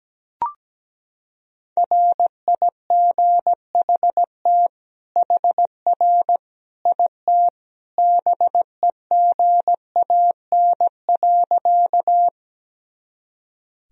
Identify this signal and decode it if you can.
Morse code, heard as a keyed tone, about 17 wpm, 700 Hz. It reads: RIGHT HR IT BEGAN.
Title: Sentences from Top 300 Words and Abbreviations 17wpm